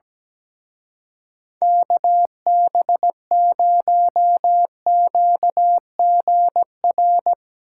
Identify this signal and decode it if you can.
Morse code, heard as a keyed tone, about 20 wpm, 700 Hz. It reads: KB0QGR
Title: Speed Racing - USA Call Signs 17wpm